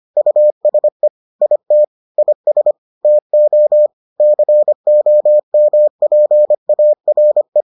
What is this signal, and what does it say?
Morse code, heard as a keyed tone, about 25 wpm, 600 Hz. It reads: USE IT IS TO COMPARE